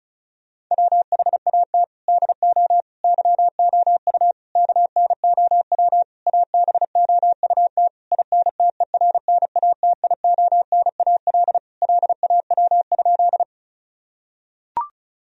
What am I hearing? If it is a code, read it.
Morse code, heard as a keyed tone, about 36 wpm, 700 Hz. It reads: WHAT DO YOU KNOW ABOUT INTERNATIONAL LAW?
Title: Sentences from Top 1000 Words - Review Part 4 - 35wpm